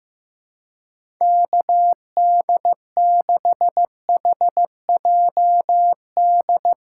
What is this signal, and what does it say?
Morse code, heard as a keyed tone, about 15 wpm, 700 Hz. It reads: KD6HJD
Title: USA Call Signs - 6 Characters - 4x Character Spacing - 15wpm